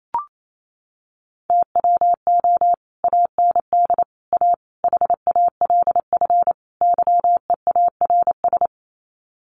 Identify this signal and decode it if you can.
Morse code, heard as a keyed tone, about 28 wpm, 700 Hz. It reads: TWO AND A HALF YEARS
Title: Sets of 5 Words 28wpm